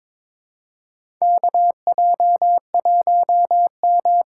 Morse code, heard as a keyed tone, about 22 wpm, 700 Hz. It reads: KJ1M
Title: USA Call Signs - 4 Characters - 2x Character Spacing - 22wpm